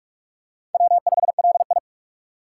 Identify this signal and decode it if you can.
Morse code, heard as a keyed tone, about 44 wpm, 700 Hz. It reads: W5LI
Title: USA Call Signs - 4 Characters - 6x Character Spacing - 45wpm